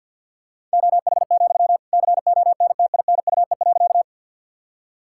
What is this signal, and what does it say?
Morse code, heard as a keyed tone, about 50 wpm, 700 Hz. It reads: OH, CONTINUE.
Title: Sentences from Top 1000 Words - Review Part 6 - 50wpm